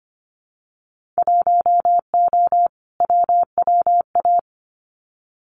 Morse code, heard as a keyed tone, about 25 wpm, 700 Hz. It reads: 1O WWA
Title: Winter Field Day Exchanges 25wpm